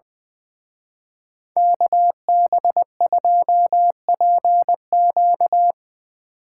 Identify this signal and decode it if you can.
Morse code, heard as a keyed tone, about 24 wpm, 700 Hz. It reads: KB2PQ
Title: Speed Racing - USA 2x2 Call Signs 20wpm